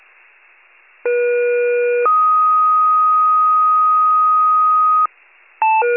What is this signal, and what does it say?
Two-tone page, heard as a one-second tone, then a three-second tone, about 480 then 1250 Hz. It is heard from 1.0 to 5.1 s.
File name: pager / 23 FIRE / wcalert